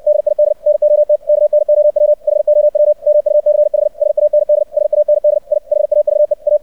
Morse code, heard as a keyed tone, about 64 wpm, 600 Hz. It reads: DID NOT KNOW HOW MUCH SAND SAND I HAVE A
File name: archived / CRSnet